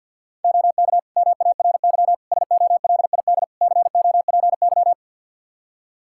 Morse code, heard as a keyed tone, about 47 wpm, 700 Hz. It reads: OK MARY SOLID COPY
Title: Practice QSO - Exchange 3 - More Info - 50wpm